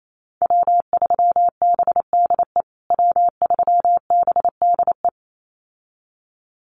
Morse code, heard as a keyed tone, about 28 wpm, 700 Hz. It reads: W3BDE W3BDE
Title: USA Call Signs Repeated 2x 28wpm